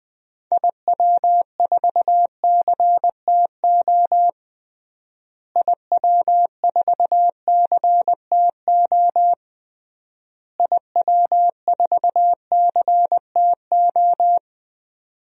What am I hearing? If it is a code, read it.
Morse code, heard as a keyed tone, about 20 wpm, 700 Hz. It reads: IW4CTO IW4CTO IW4CTO
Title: DXCC-100 Call Signs - 6 Characters - Repeated 3x - 20wpm